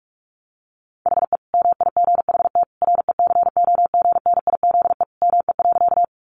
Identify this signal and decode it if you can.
Morse code, heard as a keyed tone, about 44 wpm, 700 Hz. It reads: HE MIGHT RECOGNIZE ME.
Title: Sentences from Top 1400 Words 45wpm